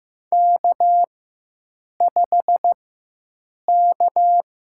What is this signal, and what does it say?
Morse code, heard as a keyed tone, about 15 wpm, spaced as for 7 wpm, 700 Hz. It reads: K5K
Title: USA Call Signs - 5 Characters - 4x Character Spacing - 15wpm